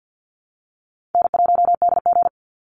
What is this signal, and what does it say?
Morse code, heard as a keyed tone, about 50 wpm, 700 Hz. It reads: N1DG
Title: Call Signs Repeated 2x - 50wpm